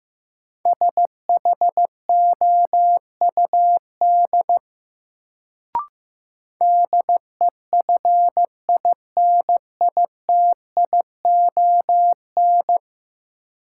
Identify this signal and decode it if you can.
Morse code, heard as a keyed tone, about 15 wpm, 700 Hz. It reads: SHOUD DEFINITION
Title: Sentences from Top 1000 Words - Review Part 4 - New Word Drill - 15wpm